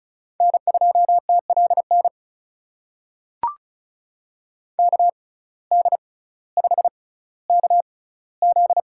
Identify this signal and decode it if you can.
Morse code, heard as a keyed tone, about 34 wpm, spaced as for 18 wpm, 700 Hz. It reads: N2TLN KD5KZ
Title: USA Call Signs - 6x Character Spacing - 35wpm